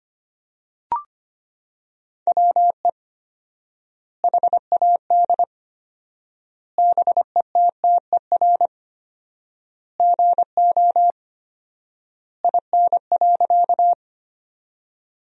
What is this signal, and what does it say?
Morse code, heard as a keyed tone, about 25 wpm, 700 Hz. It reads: WE HAD BETTER GO IN.
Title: 4x Word-Spacing - Sentences from Top 400 Words - 25wpm